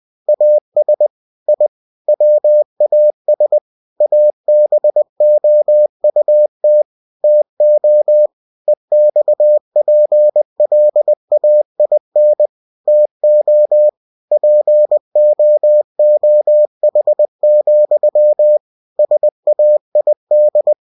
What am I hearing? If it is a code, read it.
Morse code, heard as a keyed tone, about 20 wpm, 600 Hz. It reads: AS I WAS ABOUT TO EXPLAIN TO POOH, SAID
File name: cw-books / winnie-the-pooh / winnie-20wpm-page-0082